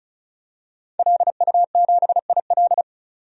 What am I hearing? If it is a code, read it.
Morse code, heard as a keyed tone, about 35 wpm, 700 Hz. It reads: LU7IL
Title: DXCC-100 Call Signs - 5 Characters - 35wpm